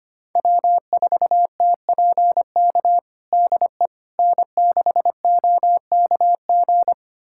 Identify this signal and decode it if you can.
Morse code, heard as a keyed tone, about 25 wpm, 700 Hz. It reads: W4TPK DE N6OKG
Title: USA Callsign DE Callsign - 5 Characters - 25wpm